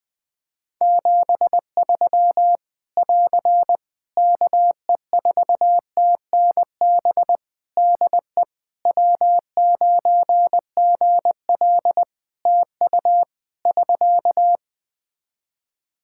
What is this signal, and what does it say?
Morse code, heard as a keyed tone, about 20 wpm, 700 Hz. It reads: 73 <AR> KE4TNB DE W9GL TU <SK>
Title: K1USN SST QSO Practice 20wpm